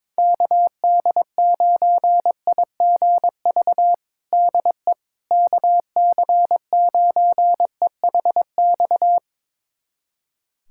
Morse code, heard as a keyed tone, about 22 wpm, 700 Hz. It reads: KD9IGV DE KC9EHX
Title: USA Callsign DE Callsign - 6 Characters - 22wpm